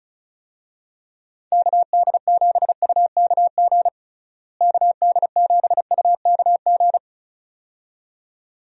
Morse code, heard as a keyed tone, about 34 wpm, 700 Hz. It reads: KD7UKG KD7UKG
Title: USA Call Signs - 6 Characters - Repeated 2x - 35wpm